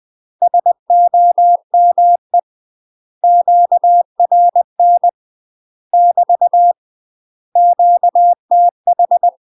Morse code, heard as a keyed tone, about 20 wpm, 700 Hz. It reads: SOME QRN <BT> QTH
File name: UK MASTER QSOs TRAINING / 20WPM / UK MASTER QSOs 0050